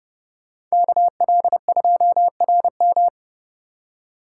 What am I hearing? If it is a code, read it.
Morse code, heard as a keyed tone, about 38 wpm, 700 Hz. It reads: KL2RM